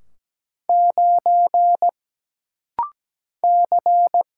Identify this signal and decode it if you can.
Morse code, heard as a keyed tone, about 17 wpm, 700 Hz. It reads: 9 C
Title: Single Letter-Number - Rapid-Fire 17wpm